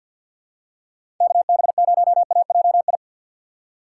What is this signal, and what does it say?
Morse code, heard as a keyed tone, about 50 wpm, 700 Hz. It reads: KB0AJI
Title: USA Call Signs - 6 Characters - 6x Character Spacing - 50wpm